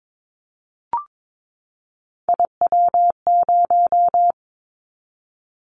Morse code, heard as a keyed tone, about 22 wpm, 700 Hz. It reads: IW0
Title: Callsign Prefixes for Top 50 DXCC - 22wpm